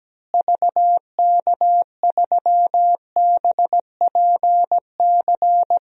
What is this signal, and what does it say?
Morse code, heard as a keyed tone, about 17 wpm, 700 Hz. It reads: VK3BPC
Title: DXCC-100 Call Signs - 6 Characters - Repeated 3x - 17wpm